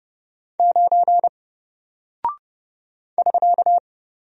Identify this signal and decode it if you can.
Morse code, heard as a keyed tone, about 30 wpm, 700 Hz. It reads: 9 <SK>
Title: Single Character (Letter:Number:Punctuation:Prosign) 30wpm